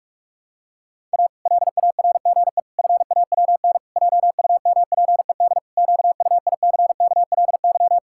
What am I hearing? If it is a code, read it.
Morse code, heard as a keyed tone, about 46 wpm, 700 Hz. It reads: A LARGE FAWN JUMPED QUICKLY
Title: Pangram Challenge v2 - 45wpm